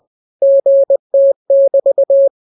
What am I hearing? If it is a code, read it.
Morse code, heard as a keyed tone, about 20 wpm, 550 Hz. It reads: GT<BT>